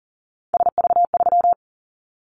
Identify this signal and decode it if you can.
Morse code, heard as a keyed tone, about 40 wpm, 700 Hz. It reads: SV3
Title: Callsign Prefixes for Top 50 DXCC - 40wpm